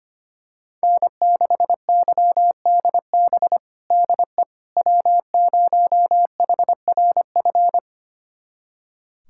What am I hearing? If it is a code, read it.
Morse code, heard as a keyed tone, about 25 wpm, 700 Hz. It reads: N6YDB DE W0HRF
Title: USA Callsign DE Callsign - 5 Characters - 25wpm